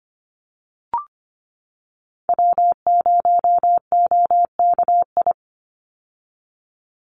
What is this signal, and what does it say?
Morse code, heard as a keyed tone, about 25 wpm, 700 Hz. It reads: W0OKI
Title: USA Call Signs 25wpm